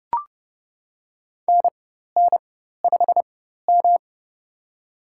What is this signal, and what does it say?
Morse code, heard as a keyed tone, about 32 wpm, spaced as for 14 wpm, 700 Hz. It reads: NN5M
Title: USA Call Signs - 4 Characters - 4x Character Spacing - 30wpm